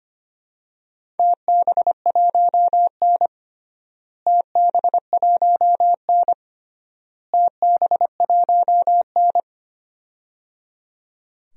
Morse code, heard as a keyed tone, about 25 wpm, 700 Hz. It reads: TB1N TB1N TB1N
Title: DXCC-100 Call Signs - 4 Characters - Repeated 3x - 25wpm